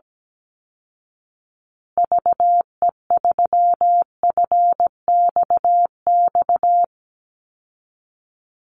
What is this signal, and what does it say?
Morse code, heard as a keyed tone, about 17 wpm, 700 Hz. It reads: VE3FXX
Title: Call Signs Repeated 2x - 17wpm